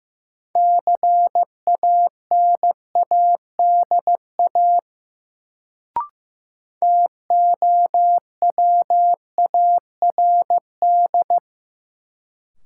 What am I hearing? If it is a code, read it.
Morse code, heard as a keyed tone, about 15 wpm, 700 Hz. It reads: CANADA TOWARD
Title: Sets of 5 Words - New Word Drill - 15wpm